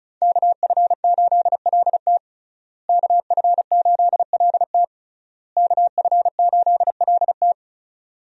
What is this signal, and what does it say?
Morse code, heard as a keyed tone, about 34 wpm, 700 Hz. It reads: KF8LT KF8LT KF8LT
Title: USA Call Signs - 5 Characters - Repeated 3x - 35wpm